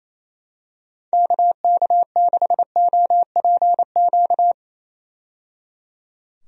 Morse code, heard as a keyed tone, about 28 wpm, 700 Hz. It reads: KK6OPQ
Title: USA Call Signs - 6 Characters - 28wpm